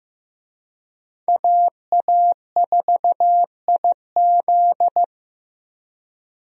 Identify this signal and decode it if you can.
Morse code, heard as a keyed tone, about 15 wpm, 700 Hz. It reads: AA4IZ
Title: USA Call Signs - 5 Characters - Repeated 3x - 15wpm